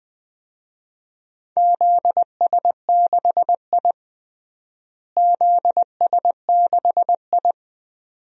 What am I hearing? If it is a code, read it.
Morse code, heard as a keyed tone, about 20 wpm, 700 Hz. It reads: ZS6I ZS6I
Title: DXCC-100 Call Signs - 4 Characters - Repeated 2x - 20wpm